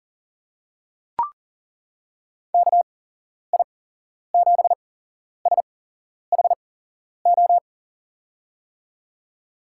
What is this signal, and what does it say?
Morse code, heard as a keyed tone, about 40 wpm, spaced as for 11 wpm, 700 Hz. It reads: KI7SHO
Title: USA Call Signs - 6 Characters - 8x Character Spacing - 40wpm